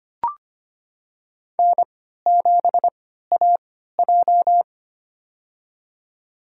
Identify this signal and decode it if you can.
Morse code, heard as a keyed tone, about 25 wpm, spaced as for 14 wpm, 700 Hz. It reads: N7AJ
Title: USA Call Signs - 3x Character Spacing - 25wpm